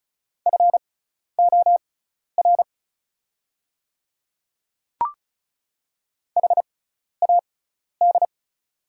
Morse code, heard as a keyed tone, about 36 wpm, spaced as for 12 wpm, 690 Hz. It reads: FOR HAD
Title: 6x Character Spacing - Top 100 Words - 35wpm